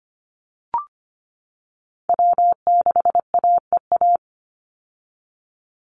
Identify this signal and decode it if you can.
Morse code, heard as a keyed tone, about 25 wpm, 700 Hz. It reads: W6AEA
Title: Call Signs - 25wpm